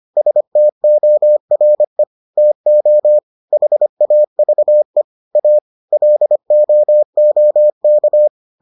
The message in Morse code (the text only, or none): STORE TO HAVE A LOOK